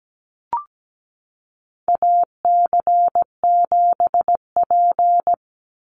Morse code, heard as a keyed tone, about 17 wpm, 700 Hz. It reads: AC7P
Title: NA SOTA Chaser Callsigns 17wpm